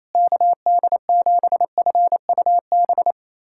Morse code, heard as a keyed tone, about 28 wpm, 700 Hz. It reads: KD7FUB
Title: USA Callsign DE Callsign - 6 Characters - 28wpm